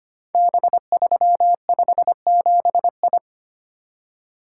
Morse code, heard as a keyed tone, about 25 wpm, 700 Hz. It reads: B357I
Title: Code Groups - 5 Letter:Numbers - 25wpm